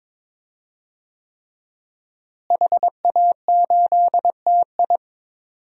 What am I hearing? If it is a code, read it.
Morse code, heard as a keyed tone, about 22 wpm, 700 Hz. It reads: HA8TI
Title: DXCC-100 Call Signs - 5 Characters - Repeated 3x - 22wpm